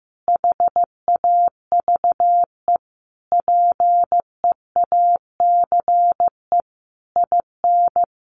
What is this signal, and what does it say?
Morse code, heard as a keyed tone, about 15 wpm, 700 Hz. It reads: HAVE PEACE IN
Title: Sentences from Top 1500 Words 15wpm